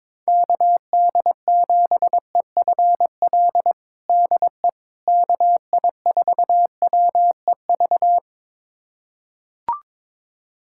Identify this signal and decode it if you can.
Morse code, heard as a keyed tone, about 22 wpm, 700 Hz. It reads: KD7EFL DE KI4WEV
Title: USA Callsign DE Callsign - 6 Characters - 22wpm